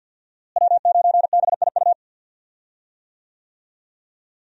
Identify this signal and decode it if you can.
Morse code, heard as a keyed tone, about 53 wpm, 700 Hz. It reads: W9BIU
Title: USA Call Signs - 5 Characters - Repeated 3x - 50wpm